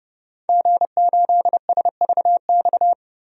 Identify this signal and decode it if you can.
Morse code, heard as a keyed tone, about 31 wpm, 700 Hz. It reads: G8SVX
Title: DXCC-100 Call Signs - 5 Characters - 30wpm